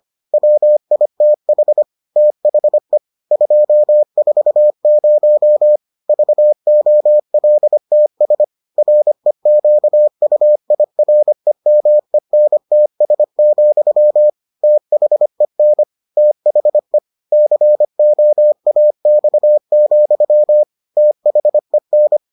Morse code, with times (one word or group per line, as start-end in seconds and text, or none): 0.3-1.8 WITH
2.2-3.0 THE
3.3-5.8 240
6.1-8.4 VOLTS
8.8-14.3 REQUIREMENTS,
14.6-15.8 THEN
16.2-17.0 THE
17.3-20.6 COAX,
21.0-22.2 THEN